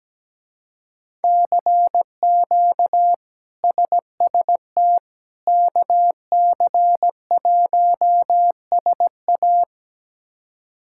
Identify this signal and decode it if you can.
Morse code, heard as a keyed tone, about 17 wpm, 700 Hz. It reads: CQ SST KC1SA
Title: K1USN SST QSO Practice - 17wpm